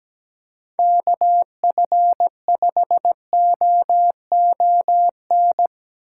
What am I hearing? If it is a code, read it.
Morse code, heard as a keyed tone, about 21 wpm, 700 Hz. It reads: KF5OON
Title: Speed Racing - USA 2x3 Call Signs 17wpm